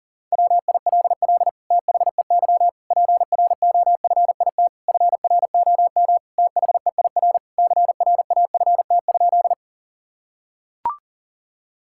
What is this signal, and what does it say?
Morse code, heard as a keyed tone, about 37 wpm, 700 Hz. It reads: WILL THEY PROFIT FROM THEIR CRAFT?
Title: Sentences from Top 1000 Words - Review Part 6 - 40wpm